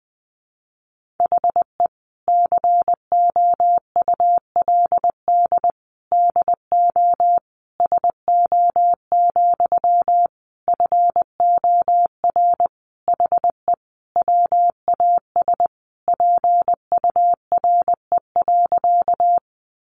Morse code, heard as a keyed tone, about 20 wpm, 700 Hz. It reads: HE COULD DO SO, FOR HE WAS PURE.
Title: Sentences from Top 1500 Words 20wpm